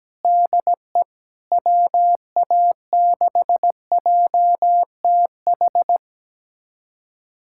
Morse code, heard as a keyed tone, about 17 wpm, 700 Hz. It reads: DE WA6JTH
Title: USA Callsign DE Callsign 17wpm